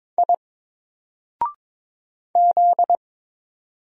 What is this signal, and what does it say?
Morse code, heard as a keyed tone, about 22 wpm, 700 Hz. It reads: I Z
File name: Single-Letter-Number-22wpm